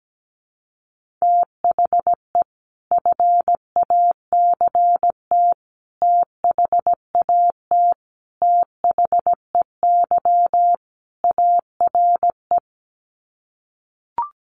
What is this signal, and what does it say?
Morse code, heard as a keyed tone, about 17 wpm, 700 Hz. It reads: THE FACT THAT THEY ARE
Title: Sets of 5 Words Encore2 17wpm